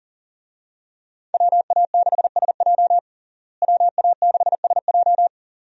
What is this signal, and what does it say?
Morse code, heard as a keyed tone, about 37 wpm, 700 Hz. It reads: WA6SJ WA6SJ
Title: USA Call Signs - 5 Characters - Repeated 2x - 40wpm